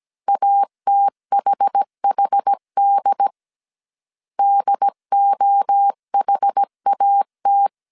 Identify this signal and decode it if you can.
Morse code, heard as a keyed tone, about 17 wpm, 800 Hz. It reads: ATHHD DOHAT
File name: CW / DET / 1 3 1500C S A 16wpm 2141